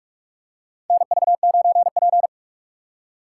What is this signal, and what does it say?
Morse code, heard as a keyed tone, about 47 wpm, 700 Hz. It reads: NV9P
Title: USA Call Signs - 2x Character Spacing - 45wpm